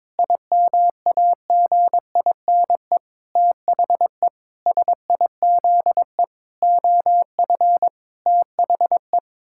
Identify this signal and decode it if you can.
Morse code, heard as a keyed tone, about 22 wpm, 700 Hz. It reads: IMAGINE THE SIZE OF THE
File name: Sentences with 7-Word Phrases - 22wpm